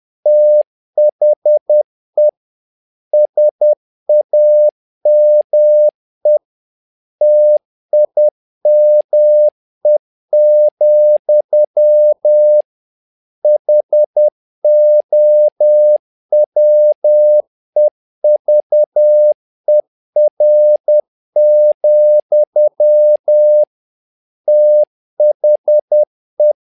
Morse code, heard as a keyed tone, about 10 wpm, 600 Hz. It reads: THE SAME TIME, HOWEVER, THE
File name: learning center / tower trouble / tower trouble 10wpm